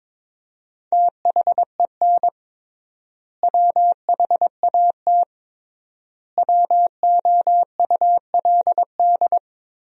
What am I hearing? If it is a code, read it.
Morse code, heard as a keyed tone, about 22 wpm, 700 Hz. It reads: THEN WHAT WOULD
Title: 3x Word-Spacing - Sentences from Top 400 Words - 22wpm